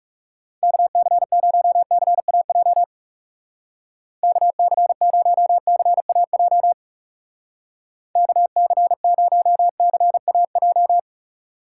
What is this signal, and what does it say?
Morse code, heard as a keyed tone, about 38 wpm, 700 Hz. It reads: KC0CAJ KC0CAJ KC0CAJ